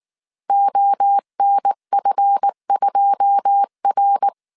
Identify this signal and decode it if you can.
Morse code, heard as a keyed tone, about 19 wpm, 800 Hz. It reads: ONF2R
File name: CW / DET / 1 9 1500C S B 18wpm 2551